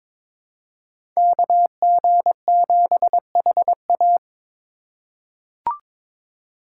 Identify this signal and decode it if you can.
Morse code, heard as a keyed tone, about 22 wpm, 700 Hz. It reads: KG7HA
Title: USA Call Signs - 5 Characters - 6x Character Spacing - 22wpm